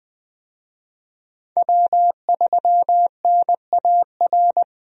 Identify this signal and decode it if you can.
Morse code, heard as a keyed tone, about 20 wpm, 700 Hz. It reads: W3NAR
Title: USA Call Signs - 5 Characters - Repeated 3x - 20wpm